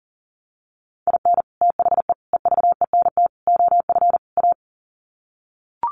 Morse code, heard as a keyed tone, about 37 wpm, 700 Hz. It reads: IN THE EVENT OF A
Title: Sets of 5 Words Encore 40wpm